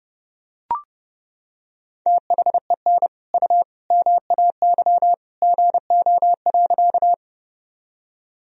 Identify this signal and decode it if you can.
Morse code, heard as a keyed tone, about 32 wpm, 700 Hz. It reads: THEN U MAY GO.
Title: Sentences from Top 100 Words and Abbreviations 30wpm